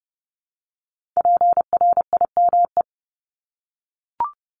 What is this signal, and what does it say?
Morse code, heard as a keyed tone, about 30 wpm, 700 Hz. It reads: PRIME
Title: Multi-letter Phonemes Repeated 3x 30wpm